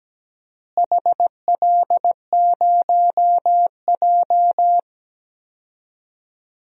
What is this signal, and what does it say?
Morse code, heard as a keyed tone, about 17 wpm, 700 Hz. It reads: HL0J
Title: DXCC-100 Call Signs - 4 Characters - Repeated 3x - 17wpm